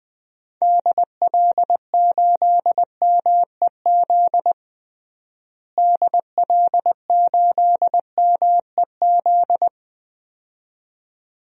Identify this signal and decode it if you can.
Morse code, heard as a keyed tone, about 20 wpm, 700 Hz. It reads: DL8MEZ DL8MEZ
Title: DXCC-100 Call Signs - 6 Characters - Repeated 2x - 20wpm